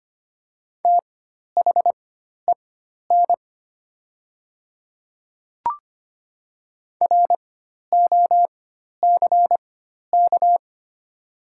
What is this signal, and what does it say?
Morse code, heard as a keyed tone, about 25 wpm, spaced as for 12 wpm, 690 Hz. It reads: THEN ROCK
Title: 4x Character Spacing - Top 500 Words - 25wpm